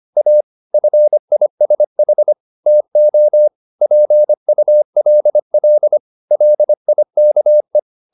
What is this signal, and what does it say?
Morse code, heard as a keyed tone, about 25 wpm, 600 Hz. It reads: A FISH TO PULL LIKE
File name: the old man and the sea part 1 25wpm